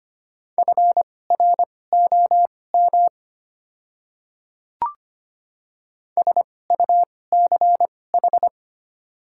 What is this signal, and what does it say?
Morse code, heard as a keyed tone, about 25 wpm, spaced as for 18 wpm, 700 Hz. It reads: FROM SUCH
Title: 2x Character Spacing - Top 300 Words - 25wpm